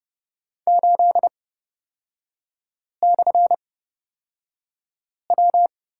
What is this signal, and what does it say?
Morse code, heard as a keyed tone, about 31 wpm, spaced as for 5 wpm, 690 Hz. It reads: 8/W
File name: Lesson 37e - Letter x - Three Characters 30wpm